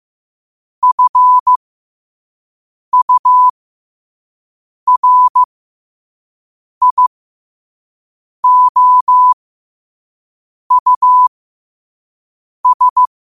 Morse code, heard as a keyed tone, about 15 wpm, spaced as for 5 wpm, 1000 Hz. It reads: FURIOUS
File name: qotdfiles / 5wpm